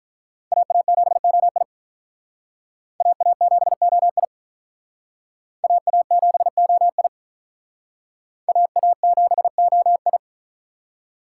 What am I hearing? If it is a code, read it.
Morse code, heard as a keyed tone, about 46 wpm, 700 Hz. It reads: AA7OI AA7OI AA7OI AA7OI